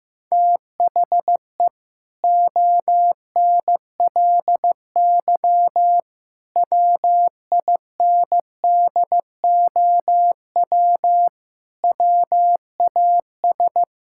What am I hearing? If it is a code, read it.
Morse code, heard as a keyed tone, about 15 wpm, 700 Hz. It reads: THE ONLY WINDOW WAS
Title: Sentences from Top 1600 Words 15wpm